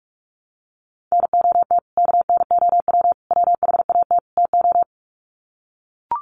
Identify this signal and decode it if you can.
Morse code, heard as a keyed tone, about 44 wpm, 700 Hz. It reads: NOT KNOW WHAT TO
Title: Sets of 4 Words 45wpm